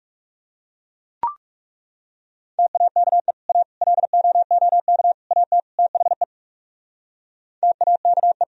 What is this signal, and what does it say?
Morse code, heard as a keyed tone, about 37 wpm, 700 Hz. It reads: TAKE A LOOK AT THE TAKE
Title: Speed-Racing - Sets of 5 Words 30wpm